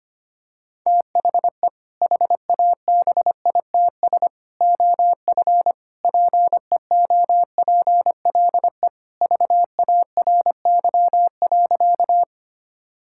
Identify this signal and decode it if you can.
Morse code, heard as a keyed tone, about 25 wpm, 700 Hz. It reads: THE HABITS OF PEOPLE VARY.
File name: practice / 25wpm / Sentences with 2-Word Phrases - 25wpm